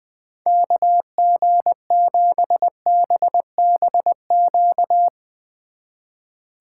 Morse code, heard as a keyed tone, about 23 wpm, 700 Hz. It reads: KG7BBQ
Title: Speed Racing - USA 2x3 Call Signs 20wpm